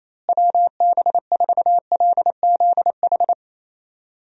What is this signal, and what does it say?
Morse code, heard as a keyed tone, about 27 wpm, 700 Hz. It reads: WB4LZH
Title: USA Call Signs - 6 Characters - Repeated 3x - 28wpm